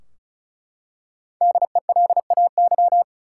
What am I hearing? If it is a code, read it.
Morse code, heard as a keyed tone, about 36 wpm, 700 Hz. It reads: DELAY
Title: Five Letter Words - 35wpm